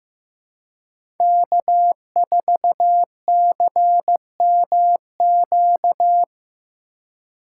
Morse code, heard as a keyed tone, about 15 wpm, 700 Hz. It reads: K4CMQ
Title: USA Call Signs - 3x Character Spacing - 15wpm